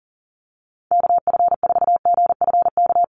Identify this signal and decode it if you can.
Morse code, heard as a keyed tone, about 40 wpm, 700 Hz. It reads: KF4GFK
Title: USA Call Signs Repeated 2x 40wpm